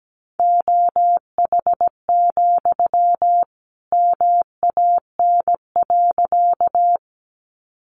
Morse code, heard as a keyed tone, about 17 wpm, 700 Hz. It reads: OH, MAN.
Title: Sentences from Top 500 Words 17wpm